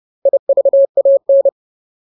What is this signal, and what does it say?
Morse code, heard as a keyed tone, about 30 wpm, 550 Hz. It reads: IVAN
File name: Ultimate Word List / 30WPM Ultimate